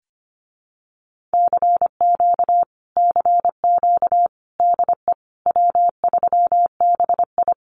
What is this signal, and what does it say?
Morse code, heard as a keyed tone, about 25 wpm, 700 Hz. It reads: CQ CQ DE W3BI
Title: Common QSO Phrases Encore 25wpm